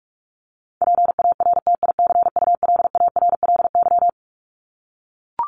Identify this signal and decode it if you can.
Morse code, heard as a keyed tone, about 44 wpm, 700 Hz. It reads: PARTICULARLY
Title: LY Suffix - Top 100 Words with Base Words 45wpm